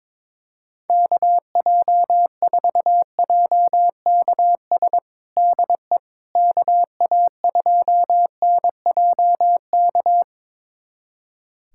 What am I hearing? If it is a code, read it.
Morse code, heard as a keyed tone, about 22 wpm, 700 Hz. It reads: KJ4JKS DE KA2NJK
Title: USA Callsign DE Callsign - 6 Characters - 22wpm